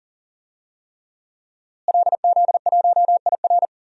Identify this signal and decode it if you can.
Morse code, heard as a keyed tone, about 40 wpm, 700 Hz. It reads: LZ1IR
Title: DXCC-100 Call Signs - 5 Characters - Repeated 3x - 40wpm